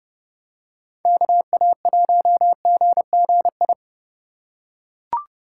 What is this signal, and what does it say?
Morse code, heard as a keyed tone, about 30 wpm, 700 Hz. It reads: KA1GGI